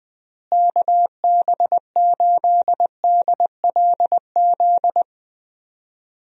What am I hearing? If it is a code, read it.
Morse code, heard as a keyed tone, about 20 wpm, 700 Hz. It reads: KB8DLZ
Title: USA Call Signs - 6 Characters - Repeated 3x - 20wpm